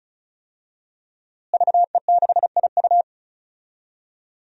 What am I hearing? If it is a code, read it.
Morse code, heard as a keyed tone, about 36 wpm, 700 Hz. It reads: VE6IU
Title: DXCC-100 Call Signs - 5 Characters - Repeated 3x - 35wpm